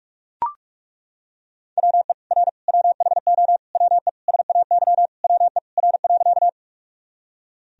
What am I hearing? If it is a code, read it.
Morse code, heard as a keyed tone, about 42 wpm, 700 Hz. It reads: WE R WHO WE SAY WE R.
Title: Sentences from Top 200 Words and Abbreviations 45wpm